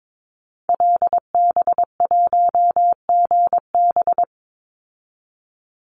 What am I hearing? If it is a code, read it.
Morse code, heard as a keyed tone, about 22 wpm, 700 Hz. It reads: LB1GB
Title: Call Signs Repeated 3x - 22wpm